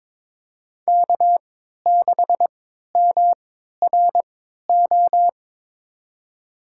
Morse code, heard as a keyed tone, about 22 wpm, spaced as for 13 wpm, 700 Hz. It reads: K6MRO
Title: USA Call Signs - 5 Characters - 3x Character Spacing - 22wpm